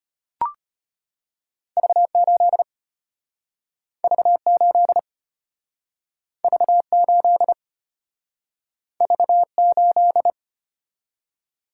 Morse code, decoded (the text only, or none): V8 V8 V8 V8